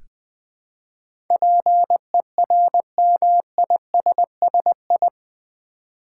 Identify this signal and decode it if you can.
Morse code, heard as a keyed tone, about 20 wpm, 700 Hz. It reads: PERMISSI
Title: Build A Word - 4 Letters At A Time - 10-Letter Words - 20wpm